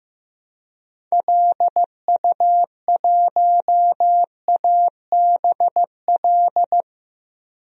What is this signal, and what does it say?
Morse code, heard as a keyed tone, about 15 wpm, 700 Hz. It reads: LU1ABL
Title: DXCC-100 Call Signs - 6 Characters - Repeated 3x - 15wpm